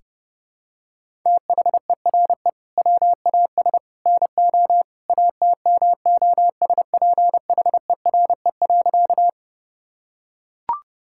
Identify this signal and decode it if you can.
Morse code, heard as a keyed tone, about 30 wpm, 700 Hz. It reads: THERE WAS NO ATMOSPHERE.
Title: Sentences from Top 1600 Words 30wpm